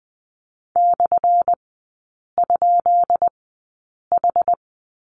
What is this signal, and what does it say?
Morse code, heard as a keyed tone, about 20 wpm, spaced as for 8 wpm, 700 Hz. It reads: /?H
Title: 178 Lesson 29e - Slash - Three Characters 20wpm